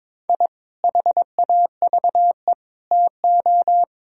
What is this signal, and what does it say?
Morse code, heard as a keyed tone, about 22 wpm, 700 Hz. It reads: I HAVE TO
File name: practice / 22wpm / Sentences with 3-Word Phrases - 22wpm